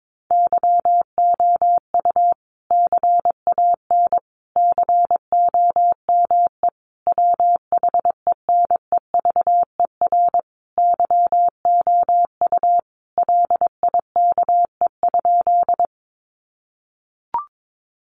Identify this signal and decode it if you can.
Morse code, heard as a keyed tone, about 22 wpm, 700 Hz. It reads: YOU CAN COME WHENEVER YOU LIKE?
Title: Sentences from Top 1200 Words 22wpm